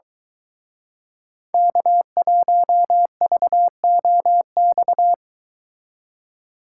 Morse code, heard as a keyed tone, about 22 wpm, 700 Hz. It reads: K1VOX